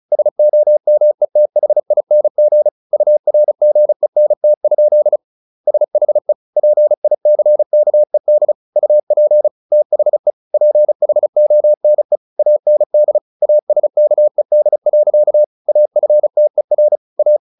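Morse code, heard as a keyed tone, about 34 wpm, 600 Hz. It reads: SOMETHING URGENT? SHE PICKED UP THE PHONE AND ASKED. AFTER A
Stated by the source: Morse practice at 35 wpm from CW med Gnister, sending Ken Follett's novel Never in English.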